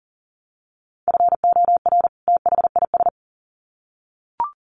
Morse code, heard as a keyed tone, about 40 wpm, 700 Hz. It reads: FOR THIS
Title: Sets of 2 Words 40wpm